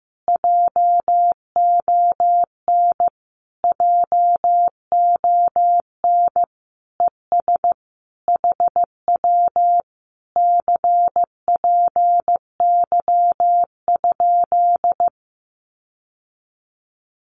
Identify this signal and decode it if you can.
Morse code, heard as a keyed tone, about 15 wpm, 700 Hz. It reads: JON JON ES HW CPY?
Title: Common QSO Phrases Encore 15wpm